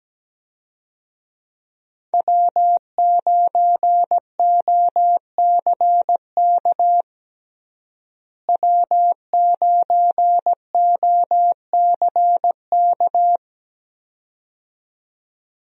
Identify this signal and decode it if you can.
Morse code, heard as a keyed tone, about 17 wpm, 700 Hz. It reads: W9OCK W9OCK
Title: USA Call Signs - 5 Characters - Repeated 2x - 17wpm